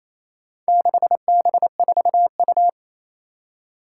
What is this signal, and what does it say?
Morse code, heard as a keyed tone, about 28 wpm, 700 Hz. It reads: 6B4U
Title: Four Challenging Letter-Number Combinations 28wpm